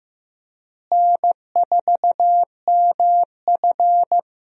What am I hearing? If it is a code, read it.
Morse code, heard as a keyed tone, about 15 wpm, 700 Hz. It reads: N4MF
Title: USA Call Signs - 4 Characters - Repeated 3x - 15wpm